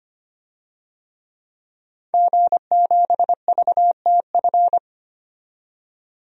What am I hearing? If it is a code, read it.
Morse code, heard as a keyed tone, about 25 wpm, 700 Hz. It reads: G7VTF
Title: DXCC-100 Call Signs - 5 Characters - 25wpm